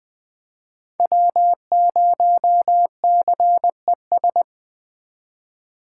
Morse code, heard as a keyed tone, about 20 wpm, 700 Hz. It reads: W0CES
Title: Speed Racing - USA 1x3 Call Signs 17wpm